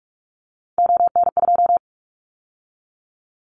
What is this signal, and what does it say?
Morse code, heard as a keyed tone, about 44 wpm, 700 Hz. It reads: ON2
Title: Callsign Prefixes for Top 100 DXCC - 45wpm